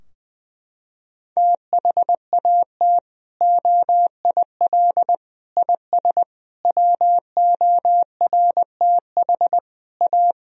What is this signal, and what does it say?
Morse code, heard as a keyed tone, about 20 wpm, 700 Hz. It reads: THAT OIL IS WORTH A
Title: Sentences from Top 1000 Words - Review Part 4 - 20wpm